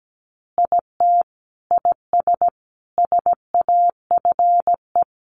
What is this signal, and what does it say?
Morse code, heard as a keyed tone, about 17 wpm, 700 Hz. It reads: IT IS SAFE
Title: Easy Phrases 17wpm